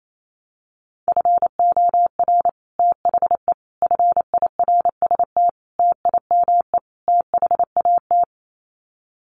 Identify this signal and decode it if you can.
Morse code, heard as a keyed tone, about 28 wpm, 700 Hz. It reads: FOR THE FIRST TIME THAT
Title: Sets of 5 Words Encore 28wpm